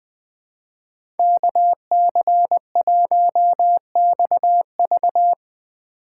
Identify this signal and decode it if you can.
Morse code, heard as a keyed tone, about 19 wpm, 700 Hz. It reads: KC1XV
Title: Speed Racing - USA 2x2 Call Signs 17wpm